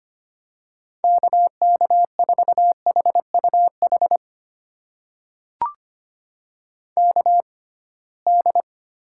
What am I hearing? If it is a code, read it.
Morse code, heard as a keyed tone, about 25 wpm, spaced as for 19 wpm, 700 Hz. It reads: KK4HUH KD
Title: USA Call Signs - 6 Characters - 6x Character Spacing - 25wpm